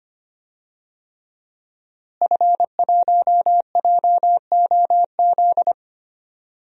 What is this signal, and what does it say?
Morse code, heard as a keyed tone, about 25 wpm, 700 Hz. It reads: F1JOZ
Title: DXCC-100 Call Signs - 5 Characters - Repeated 3x - 25wpm